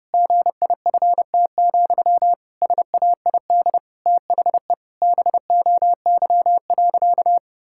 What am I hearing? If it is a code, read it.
Morse code, heard as a keyed tone, about 30 wpm, 700 Hz. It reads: GIFT, SAID THE BOY.
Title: Sentences from Top 1000 Words - Review Part 6 - 30wpm